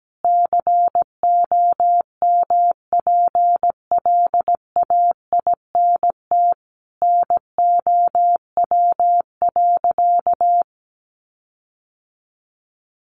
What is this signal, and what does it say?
Morse code, heard as a keyed tone, about 17 wpm, 700 Hz. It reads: COMPLAINT NOW.
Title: Sentences from Top 1400 Words 17wpm